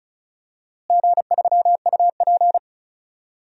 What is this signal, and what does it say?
Morse code, heard as a keyed tone, about 34 wpm, 700 Hz. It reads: G3UP
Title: DXCC-100 Call Signs - 4 Characters - Repeated 2x - 35wpm